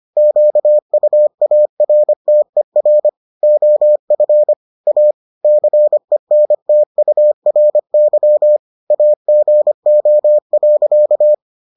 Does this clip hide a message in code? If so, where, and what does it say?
Yes: Morse code from 0.2 to 11.3 s, heard as a keyed tone, about 25 wpm, 600 Hz. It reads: QUARTER OF A CENTURY AGO.